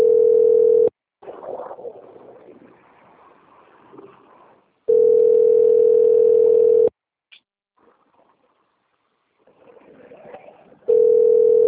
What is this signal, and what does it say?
Ringback tone (North America), 440 + 480 Hz.